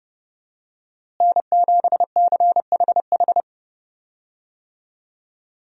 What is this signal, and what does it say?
Morse code, heard as a keyed tone, about 32 wpm, 700 Hz. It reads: N7CHH
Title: USA Call Signs - 5 Characters - Repeated 2x - 30wpm